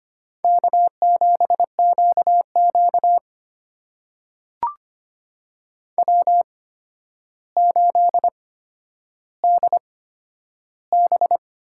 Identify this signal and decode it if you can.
Morse code, heard as a keyed tone, about 25 wpm, spaced as for 11 wpm, 700 Hz. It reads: K7QQ W8DB
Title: USA Call Signs - 4 Characters - 8x Character Spacing - 25wpm